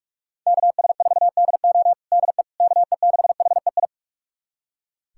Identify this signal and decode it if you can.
Morse code, heard as a keyed tone, about 41 wpm, 700 Hz. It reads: KS4DO DE KE6HEI
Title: USA Callsign DE Callsign 45wpm